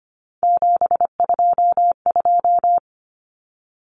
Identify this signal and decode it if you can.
Morse code, heard as a keyed tone, about 25 wpm, 700 Hz. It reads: 722
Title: Common Cut Numbers - 3-Digit 25wpm